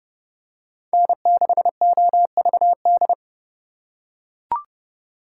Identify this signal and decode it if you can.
Morse code, heard as a keyed tone, about 28 wpm, 700 Hz. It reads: N6OVD